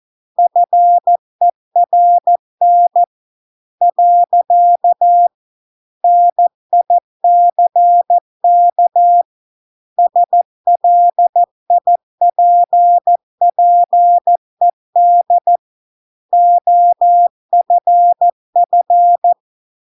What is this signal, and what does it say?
Morse code, heard as a keyed tone, about 14 wpm, 700 Hz. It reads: FERN . NICK SLIPPED OFF